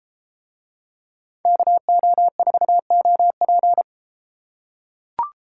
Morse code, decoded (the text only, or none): KO4OP